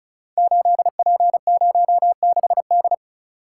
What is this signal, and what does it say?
Morse code, heard as a keyed tone, about 35 wpm, 700 Hz. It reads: 8P06D
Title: Code Groups - 5 Letter:Numbers - 35wpm